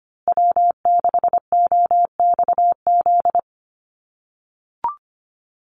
Morse code, heard as a keyed tone, about 25 wpm, 700 Hz. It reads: W6OXZ
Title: USA Call Signs Repeated 2x 25wpm